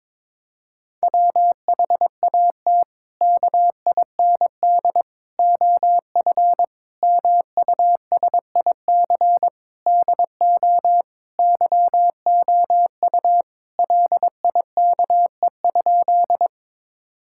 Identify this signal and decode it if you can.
Morse code, heard as a keyed tone, about 21 wpm, 700 Hz. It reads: WHAT KIND OF MUSIC DO YOU LIKE?
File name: practice / 22wpm / Sentences with 3-Word Phrases - 22wpm